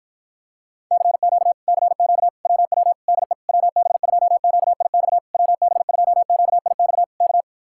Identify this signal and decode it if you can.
Morse code, heard as a keyed tone, about 54 wpm, 700 Hz. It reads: CQ CQ WW DE WB2QIX WB2QIX K